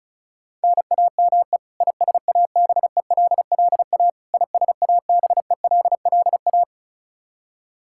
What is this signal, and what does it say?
Morse code, heard as a keyed tone, about 34 wpm, 700 Hz. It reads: NAME ISABELLA ISABELLA
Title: K1USN SST QSO Practice 35wpm